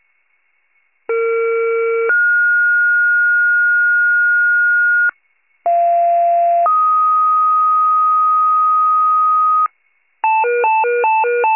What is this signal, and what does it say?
Two-tone page, heard as a one-second tone, then a three-second tone, about 460 then 1430 Hz, about 690 then 1250 Hz, with noise. It is heard from 1.1 to 9.7 s.
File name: wcalert